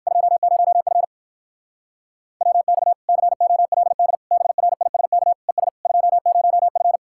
Morse code, heard as a keyed tone, about 58 wpm, 700 Hz. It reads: 20F WX COLD BRISK ES 20F